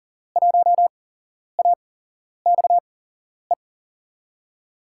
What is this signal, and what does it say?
Morse code, heard as a keyed tone, about 40 wpm, spaced as for 11 wpm, 700 Hz. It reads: 1AXE
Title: USA Call Signs - 6 Characters - 8x Character Spacing - 40wpm